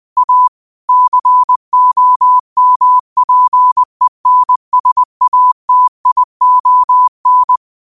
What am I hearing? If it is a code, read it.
Morse code, heard as a keyed tone, about 20 wpm, 1000 Hz. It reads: A COMPENSATION